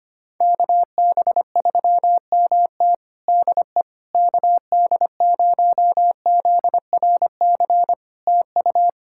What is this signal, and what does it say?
Morse code, heard as a keyed tone, about 25 wpm, 700 Hz. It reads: KB3MT DE KD0ZRC TU
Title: Practice QSO - Exchange 4 - Ending - 25wpm